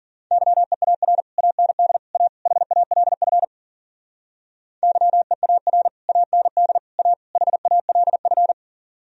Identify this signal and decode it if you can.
Morse code, heard as a keyed tone, about 52 wpm, 700 Hz. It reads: YEAR AND A HALF YEAR AND A HALF